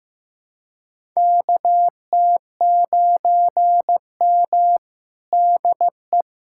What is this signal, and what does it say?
Morse code, heard as a keyed tone, about 15 wpm, 700 Hz. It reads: KT9M DE
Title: USA Callsign DE Callsign - 4 Characters - 15wpm